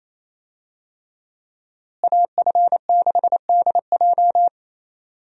Morse code, heard as a keyed tone, about 28 wpm, 700 Hz. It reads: AF6DJ
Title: USA Call Signs - 5 Characters - Repeated 3x - 28wpm